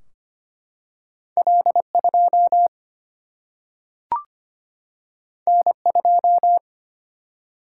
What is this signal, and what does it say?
Morse code, heard as a keyed tone, about 25 wpm, 700 Hz. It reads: L 2 N 2
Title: Single-Letter Single-Number 25wpm